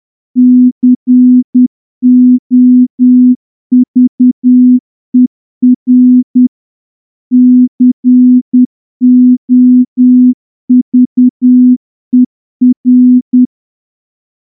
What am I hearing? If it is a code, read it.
Morse code, heard as a keyed tone, about 10 wpm, 250 Hz. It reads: COVER COVER
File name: morse-training-10WPM